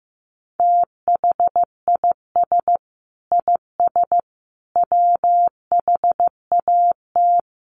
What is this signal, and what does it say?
Morse code, heard as a keyed tone, about 15 wpm, 700 Hz. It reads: THIS IS WHAT
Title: Sentences from Top 300 Words 15wpm